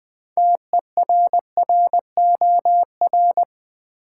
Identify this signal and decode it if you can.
Morse code, heard as a keyed tone, about 20 wpm, 700 Hz. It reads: TERROR
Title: Build A Word - 3 Letters At A Time - 9-Letter Words - 20wpm